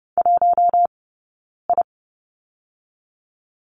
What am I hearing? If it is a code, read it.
Morse code, heard as a keyed tone, about 30 wpm, 700 Hz. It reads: 1 I
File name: Lesson 15d - Number 5 - Two Characters 30wpm